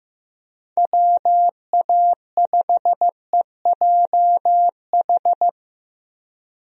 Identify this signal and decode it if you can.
Morse code, heard as a keyed tone, about 15 wpm, 700 Hz. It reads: WA5EJH
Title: USA Call Signs - 6 Characters - Repeated 2x - 15wpm